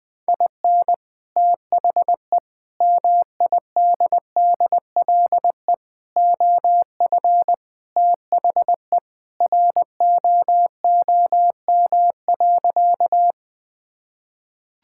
Morse code, heard as a keyed tone, about 20 wpm, 700 Hz. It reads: IN THE MIDDLE OF THE ROOM.
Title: Sentences with 2-Word Phrases - Spoken Beginning - 20wpm